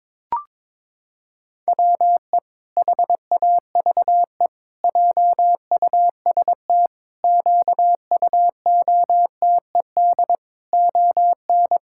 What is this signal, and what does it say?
Morse code, heard as a keyed tone, about 22 wpm, 700 Hz. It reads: WE HAVE JUST QUOTED ON
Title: Pangram Challenge v2 - 22wpm